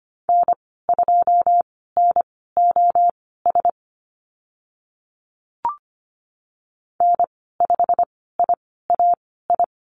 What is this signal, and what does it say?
Morse code, heard as a keyed tone, about 25 wpm, spaced as for 16 wpm, 700 Hz. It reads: N2NOS N5IAI
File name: Lesson 15g - Number 5 - Call Signs 25wpm